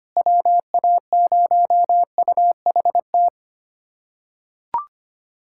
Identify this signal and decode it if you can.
Morse code, heard as a keyed tone, about 25 wpm, 700 Hz. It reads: WA0UHT
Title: USA Callsign DE Callsign - 6 Characters - 25wpm